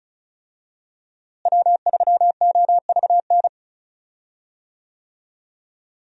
Morse code, heard as a keyed tone, about 36 wpm, 700 Hz. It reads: W3OVN
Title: USA Call Signs - 5 Characters - Repeated 3x - 35wpm